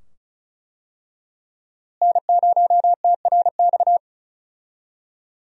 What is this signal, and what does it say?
Morse code, heard as a keyed tone, about 36 wpm, 700 Hz. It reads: N0TRX
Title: USA Call Signs - 5 Characters - Repeated 2x - 35wpm